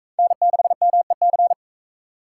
Morse code, heard as a keyed tone, about 44 wpm, 700 Hz. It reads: N6MEC